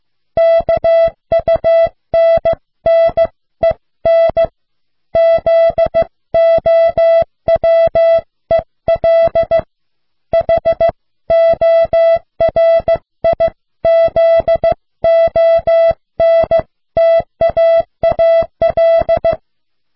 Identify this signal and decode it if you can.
Morse code, heard as a keyed tone, about 15 wpm, 660 Hz. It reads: KUNNEN ZOWEL HORIZONTAAL